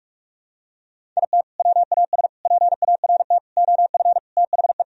Morse code, heard as a keyed tone, about 46 wpm, 700 Hz. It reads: IT WAS PART OF THE
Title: Sentences from Top 1000 Words - Review Part 4 - 45wpm